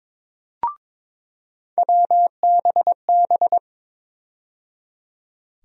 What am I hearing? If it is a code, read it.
Morse code, heard as a keyed tone, about 21 wpm, 700 Hz. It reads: WBB
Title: USA Call Sign Postfixes - 3 Letters 22wpm